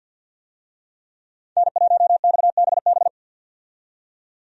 Morse code, heard as a keyed tone, about 50 wpm, 700 Hz. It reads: N1XBB
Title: USA Call Signs - 5 Characters - 50wpm